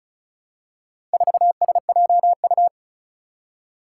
Morse code, heard as a keyed tone, about 36 wpm, 700 Hz. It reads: 4SJU
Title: Four Challenging Letter-Number Combinations 35wpm